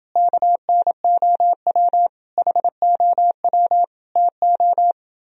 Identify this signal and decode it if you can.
Morse code, heard as a keyed tone, about 29 wpm, 700 Hz. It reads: KNOW HOW TO